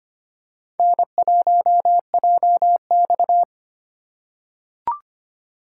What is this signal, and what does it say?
Morse code, heard as a keyed tone, about 25 wpm, 700 Hz. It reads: N1JX
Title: USA Call Signs - 4 Characters - 4x Character Spacing - 25wpm